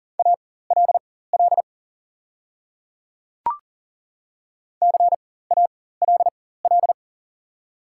Morse code, heard as a keyed tone, about 40 wpm, spaced as for 19 wpm, 700 Hz. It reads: ALL CALL
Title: 4x Character Spacing - Top 100 Words - 40wpm